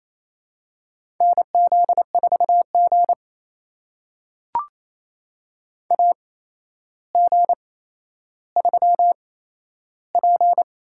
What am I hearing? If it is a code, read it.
Morse code, heard as a keyed tone, about 28 wpm, spaced as for 12 wpm, 700 Hz. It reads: NZ4G AG3P
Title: USA Call Signs - 4 Characters - 8x Character Spacing - 28wpm